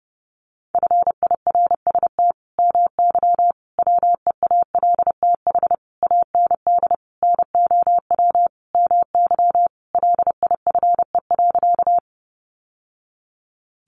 Morse code, heard as a keyed tone, about 30 wpm, 700 Hz. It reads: FIRST MY WEALTH AND NOW MY LIFE.
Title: Sentences from Top 1200 Words 30wpm